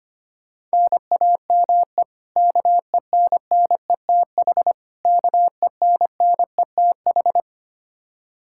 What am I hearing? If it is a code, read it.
Morse code, heard as a keyed tone, about 25 wpm, 700 Hz. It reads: NAME KENNETH KENNETH
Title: Practice QSO - Exchange 1 - Beginning - 25wpm